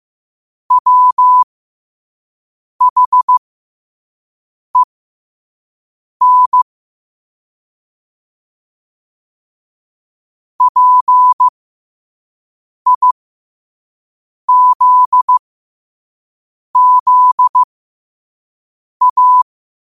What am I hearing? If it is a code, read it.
Morse code, heard as a keyed tone, about 15 wpm, spaced as for 5 wpm, 1000 Hz. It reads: WHEN PIZZA
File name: qotdfiles / 5wpm